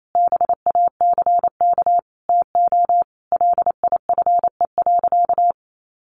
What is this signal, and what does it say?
Morse code, heard as a keyed tone, about 28 wpm, 700 Hz. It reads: BACK TO LIFE.
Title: Sentences from Top 600 Words 28wpm